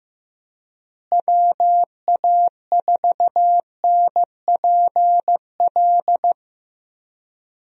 Morse code, heard as a keyed tone, about 15 wpm, 700 Hz. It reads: WA4NPL
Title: USA Call Signs - 6 Characters - Repeated 3x - 15wpm